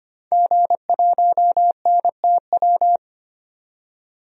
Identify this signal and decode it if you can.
Morse code, heard as a keyed tone, about 25 wpm, 700 Hz. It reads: G1NTW
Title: DXCC-100 Call Signs - 5 Characters - 25wpm